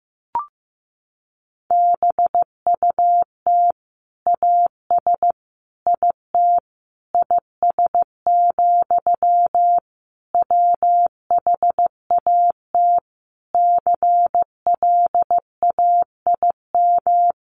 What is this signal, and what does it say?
Morse code, heard as a keyed tone, about 15 wpm, 700 Hz. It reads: BUT AS IT IS, WHAT CLAIM
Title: Sentences from Top 1500 Words 15wpm